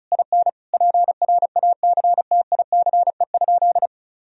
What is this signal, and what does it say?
Morse code, heard as a keyed tone, about 36 wpm, 700 Hz. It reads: IN PRACTICE?
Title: Sentences from Top 1000 Words - Review Part 4 - 35wpm